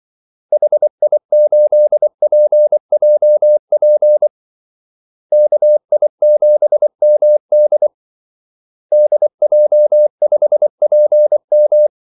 Morse code, heard as a keyed tone, about 24 wpm, 600 Hz. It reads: HI8PJP KI7MD DJ5PM
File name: Callsigns-24wpm